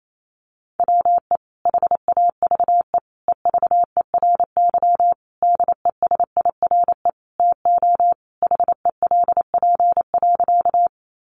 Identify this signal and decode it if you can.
Morse code, heard as a keyed tone, about 28 wpm, 700 Hz. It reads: WE HAVE EVERY DESIRE TO HELP.
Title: Sentences from Top 1100 Words 28wpm